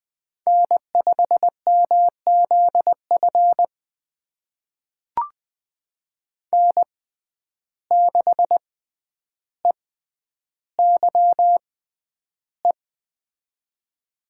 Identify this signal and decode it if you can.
Morse code, heard as a keyed tone, about 21 wpm, spaced as for 10 wpm, 700 Hz. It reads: N5MZF N6EYE
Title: USA Call Signs - 5 Characters - 6x Character Spacing - 20wpm